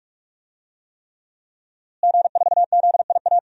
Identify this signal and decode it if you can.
Morse code, heard as a keyed tone, about 44 wpm, 700 Hz. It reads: G4ZIA
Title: DXCC-100 Call Signs - 5 Characters - Repeated 3x - 45wpm